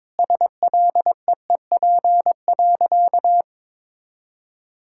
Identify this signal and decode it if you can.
Morse code, heard as a keyed tone, about 22 wpm, 700 Hz. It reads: SLEEP.
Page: morse-code-ninja-practice-22wpm